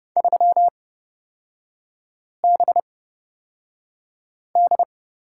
Morse code, heard as a keyed tone, about 31 wpm, spaced as for 5 wpm, 700 Hz. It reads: 3BD
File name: Lesson 38e - Letter q - Three Characters 30wpm